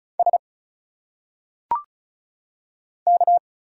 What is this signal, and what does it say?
Morse code, heard as a keyed tone, about 36 wpm, 700 Hz. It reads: S K
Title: Single Letters - Phonetic Pronunciation 35wpm